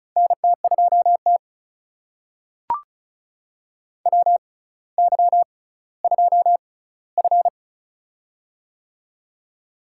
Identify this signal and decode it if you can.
Morse code, heard as a keyed tone, about 36 wpm, spaced as for 18 wpm, 700 Hz. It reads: NT2T WY2F
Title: USA Call Signs - 4 Characters - 6x Character Spacing - 35wpm